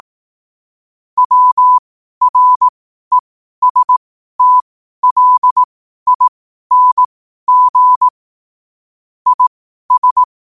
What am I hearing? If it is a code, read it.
Morse code, heard as a keyed tone, about 18 wpm, spaced as for 12 wpm, 1000 Hz. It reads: WRESTLING IS
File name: qotdfiles / 12wpm